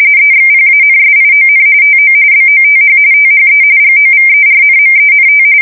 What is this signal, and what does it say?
RTTY, about 45.45 baud. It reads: EXPANSION TO BOTH THE GOVERNMENT
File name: fsk170 / usnavy3